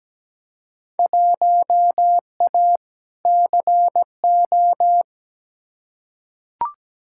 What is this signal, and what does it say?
Morse code, heard as a keyed tone, about 17 wpm, 700 Hz. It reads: 1A CO
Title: Field Day Exchanges - 17wpm